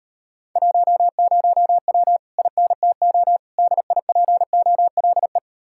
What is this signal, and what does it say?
Morse code, heard as a keyed tone, about 38 wpm, 700 Hz. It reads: 10W INTO DIPOLE